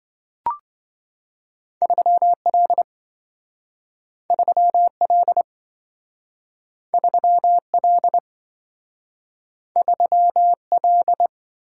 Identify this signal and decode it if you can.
Morse code, heard as a keyed tone, about 27 wpm, 700 Hz. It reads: 3L 3L 3L 3L